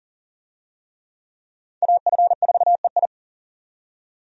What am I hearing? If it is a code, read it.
Morse code, heard as a keyed tone, about 40 wpm, 700 Hz. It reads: AF4EI
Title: USA Call Signs - 5 Characters - 40wpm